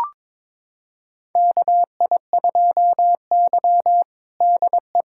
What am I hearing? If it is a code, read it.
Morse code, heard as a keyed tone, about 22 wpm, 700 Hz. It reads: KI2Y DE
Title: USA Callsign DE Callsign - 4 Characters - 22wpm